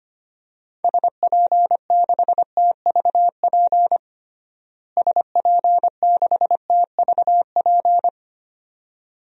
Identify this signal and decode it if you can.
Morse code, heard as a keyed tone, about 25 wpm, 700 Hz. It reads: SP6TVP SP6TVP
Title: DXCC-100 Call Signs - 6 Characters - Repeated 2x - 25wpm